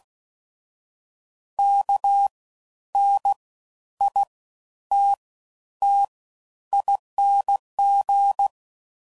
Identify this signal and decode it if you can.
Morse code, heard as a keyed tone, about 16 wpm, spaced as for 10 wpm, 800 Hz. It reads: KNITTING
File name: Suffixes / Special INGs / Fast Ings